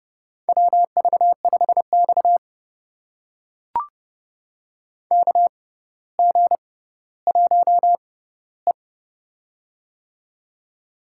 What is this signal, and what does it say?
Morse code, heard as a keyed tone, about 32 wpm, spaced as for 16 wpm, 700 Hz. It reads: WV5X KG1E
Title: USA Call Signs - 4 Characters - 6x Character Spacing - 30wpm